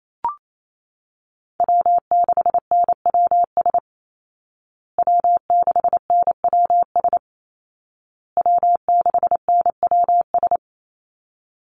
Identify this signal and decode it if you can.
Morse code, heard as a keyed tone, about 28 wpm, 700 Hz. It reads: W6NWS W6NWS W6NWS
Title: Call Signs Repeated 2x - 28wpm